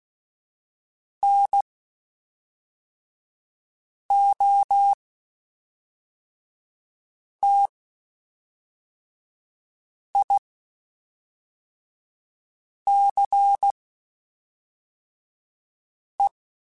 Morse code, heard as a keyed tone, about 16 wpm, spaced as for 3 wpm, 800 Hz. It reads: NOTICE
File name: QSOs / QSOs-mp3s / WD0BGP 1@5